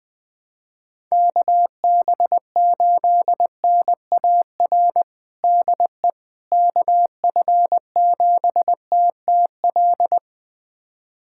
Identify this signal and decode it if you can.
Morse code, heard as a keyed tone, about 20 wpm, 700 Hz. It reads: KB8NAR DE KF7TTL
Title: USA Callsign DE Callsign - 6 Characters - 20wpm